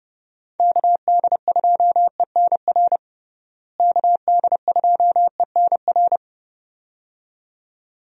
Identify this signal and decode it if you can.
Morse code, heard as a keyed tone, about 31 wpm, 700 Hz. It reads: KD2ENR KD2ENR
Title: USA Call Signs - 6 Characters - Repeated 2x - 30wpm